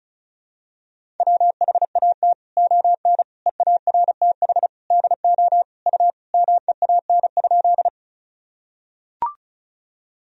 Morse code, heard as a keyed tone, about 36 wpm, 700 Hz. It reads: WHAT ON EARTH DO U MEAN?
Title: Sentences from Top 200 Words and Abbreviations 35wpm